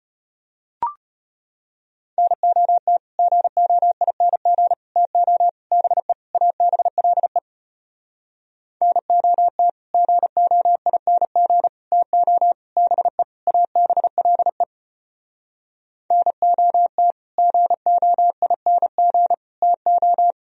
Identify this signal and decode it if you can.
Morse code, heard as a keyed tone, about 33 wpm, 700 Hz. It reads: NOT GOING TO BE ABLE NOT GOING TO BE ABLE NOT GOING TO